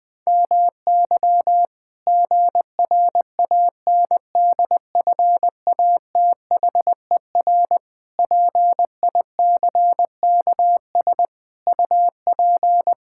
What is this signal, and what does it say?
Morse code, heard as a keyed tone, about 21 wpm, 700 Hz. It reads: MY GRANDFATHER PICKS UP
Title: Pangram Challenge v2 - 20wpm